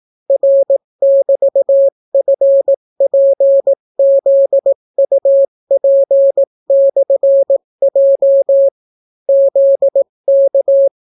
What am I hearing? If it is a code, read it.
Morse code, heard as a keyed tone, about 18 wpm, 550 Hz. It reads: R<BT>FPZUP/J ZK